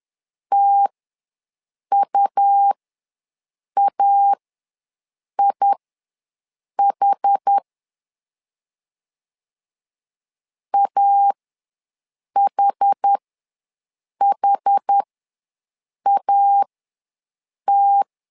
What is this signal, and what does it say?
Morse code, heard as a keyed tone, about 10 wpm, spaced as for 6 wpm, 800 Hz. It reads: TUAIH AHHAT
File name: LEZIONI / 2 UHDN / 2 UH ws 10wpm 2029